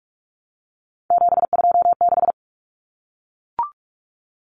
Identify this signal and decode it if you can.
Morse code, heard as a keyed tone, about 44 wpm, 700 Hz. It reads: 726
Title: Numbers - 3-Digit 45wpm